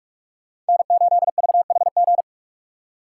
Morse code, heard as a keyed tone, about 58 wpm, 700 Hz. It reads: N8VHG